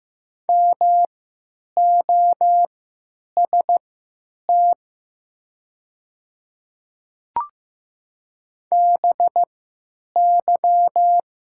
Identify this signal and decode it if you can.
Morse code, heard as a keyed tone, about 15 wpm, spaced as for 9 wpm, 700 Hz. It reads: MOST BY
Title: 3x Character Spacing - Top 200 Words - 15wpm